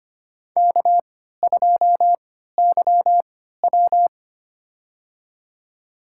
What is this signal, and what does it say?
Morse code, heard as a keyed tone, about 25 wpm, spaced as for 14 wpm, 700 Hz. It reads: K2YW
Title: USA Call Signs - 4 Characters - 3x Character Spacing - 25wpm